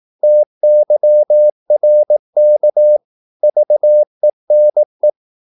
Morse code, heard as a keyed tone, about 18 wpm, 600 Hz. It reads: TYRK VENE
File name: Korte ord DK 18wpm